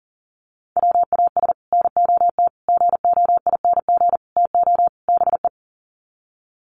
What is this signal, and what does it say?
Morse code, heard as a keyed tone, about 37 wpm, 700 Hz. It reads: WAS NOT GOING TO BE
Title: Sets of 5 Words Encore 40wpm